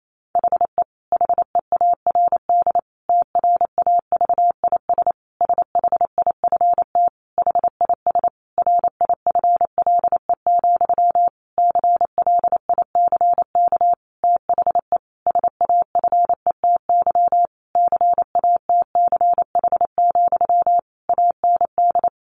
Morse code, heard as a keyed tone, about 28 wpm, 700 Hz. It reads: HE HEARD TRAVIS SHIFT HIS RIFLE, CLICK THE SAFETY CATCH, AND
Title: A Sound Of Thunder 28wpm